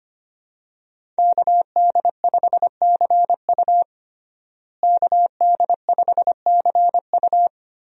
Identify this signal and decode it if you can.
Morse code, heard as a keyed tone, about 25 wpm, 700 Hz. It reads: KD5CU KD5CU
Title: USA Call Signs - 5 Characters - Repeated 3x - 25wpm